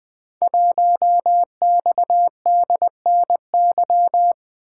Morse code, heard as a keyed tone, about 20 wpm, 700 Hz. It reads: 1XDNY
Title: Code Groups - 5 Letter:Numbers - 20wpm